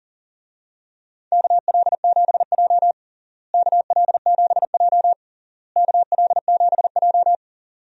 Morse code, heard as a keyed tone, about 40 wpm, 700 Hz. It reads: KL7J KL7J KL7J
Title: DXCC-100 Call Signs - 4 Characters - Repeated 3x - 40wpm